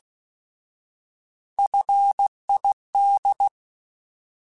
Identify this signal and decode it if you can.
Morse code, heard as a keyed tone, about 16 wpm, 800 Hz. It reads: FID
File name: Alphabetical / Alphabetical mp3s / Fs @15